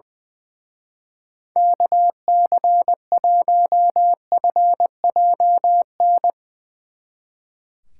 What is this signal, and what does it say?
Morse code, heard as a keyed tone, about 22 wpm, 700 Hz. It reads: KC1FJN
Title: Speed Racing - USA 2x3 Call Signs 20wpm